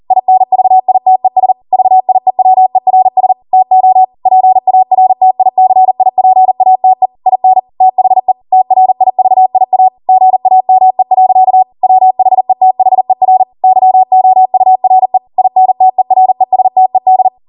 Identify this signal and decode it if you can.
Morse code, heard as a keyed tone, about 40 wpm, 750 Hz. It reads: INVITES VIEWERS TO PARTICIPATE IN THE TRIVIA GAME. WHETHER YOURE INTERESTED